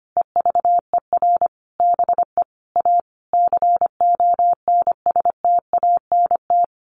Morse code, heard as a keyed tone, about 25 wpm, 700 Hz. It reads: EVER BE A CONSTANT
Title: Sentences from Top 1500 Words 25wpm